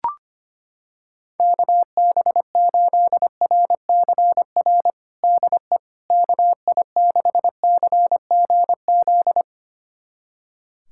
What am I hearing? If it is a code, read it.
Morse code, heard as a keyed tone, about 25 wpm, 700 Hz. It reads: KB8RCR DE KI6CGZ
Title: USA Callsign DE Callsign - 6 Characters - 25wpm